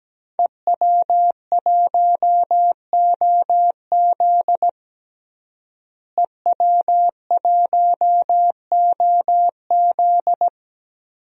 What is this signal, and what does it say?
Morse code, heard as a keyed tone, about 17 wpm, 700 Hz. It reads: EW1OZ EW1OZ
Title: DXCC-100 Call Signs - 5 Characters - Repeated 3x - 17wpm